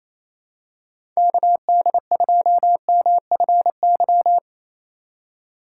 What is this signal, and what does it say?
Morse code, heard as a keyed tone, about 28 wpm, 700 Hz. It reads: KD2MFY
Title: USA Call Signs - 6 Characters - 4x Character Spacing - 28wpm